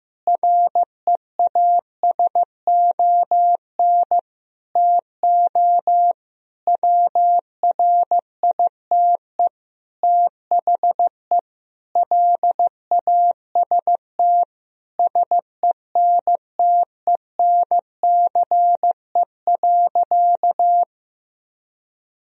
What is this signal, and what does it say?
Morse code, heard as a keyed tone, about 15 wpm, 700 Hz. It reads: REASON TO WRITE THE LAST SENTENCE.
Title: Sentences from Top 1000 Words - Review Part 1 - 15wpm